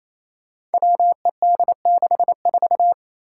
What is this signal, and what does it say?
Morse code, heard as a keyed tone, about 27 wpm, 700 Hz. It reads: WED64
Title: Five Challenging Letter-Number Combinations 28wpm